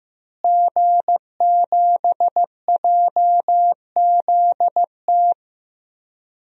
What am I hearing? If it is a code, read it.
Morse code, heard as a keyed tone, about 15 wpm, 700 Hz. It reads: G7JZT
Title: DXCC-100 Call Signs - 5 Characters - Repeated 2x - 15wpm